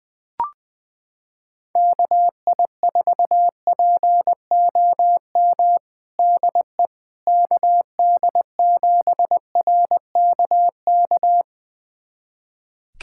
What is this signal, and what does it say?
Morse code, heard as a keyed tone, about 20 wpm, 700 Hz. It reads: KI4POM DE KD7RKK
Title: USA Callsign DE Callsign - 6 Characters - 20wpm